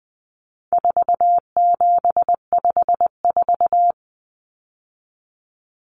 Morse code, heard as a keyed tone, about 20 wpm, 700 Hz. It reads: 4754
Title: Common Cut Numbers - 4-Digit 20wpm